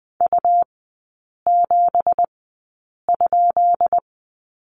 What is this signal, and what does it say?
Morse code, heard as a keyed tone, about 20 wpm, spaced as for 8 wpm, 700 Hz. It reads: U7?
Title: 178 Lesson 29e - Slash - Three Characters 20wpm